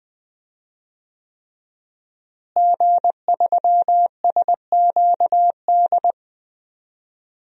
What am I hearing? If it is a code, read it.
Morse code, heard as a keyed tone, about 20 wpm, 700 Hz. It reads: G3SQD
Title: DXCC-100 Call Signs - 5 Characters - Repeated 3x - 20wpm